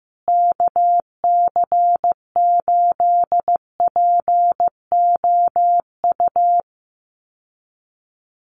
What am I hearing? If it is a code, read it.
Morse code, heard as a keyed tone, about 15 wpm, 700 Hz. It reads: KC8POU
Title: USA Call Signs Repeated 3x 15wpm